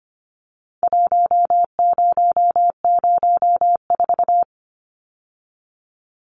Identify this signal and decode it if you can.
Morse code, heard as a keyed tone, about 25 wpm, 700 Hz. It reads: 1004
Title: Numbers - 4-Digit 25wpm